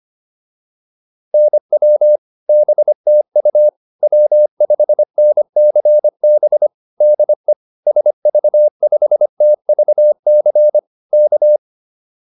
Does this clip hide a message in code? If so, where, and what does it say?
Yes: Morse code from 1.3 to 11.6 s, heard as a keyed tone, about 25 wpm, 600 Hz. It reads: NW BTU W5NCB DE SV5TVC K